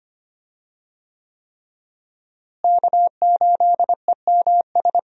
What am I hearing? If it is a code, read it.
Morse code, heard as a keyed tone, about 25 wpm, 700 Hz. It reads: K8EMS
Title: USA Call Signs - 5 Characters - Repeated 3x - 25wpm